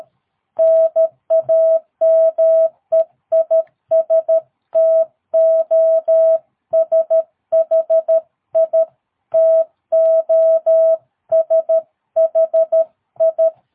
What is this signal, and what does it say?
Morse code, heard as a keyed tone, about 13 wpm, 660 Hz. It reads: NAME IS TOSHI TOSHI
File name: jh8cbh / 2022cw